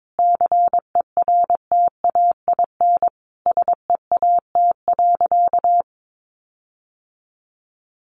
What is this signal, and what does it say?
Morse code, heard as a keyed tone, about 22 wpm, 700 Hz. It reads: CERTAIN SEAT.
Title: Sentences from Top 1500 Words 22wpm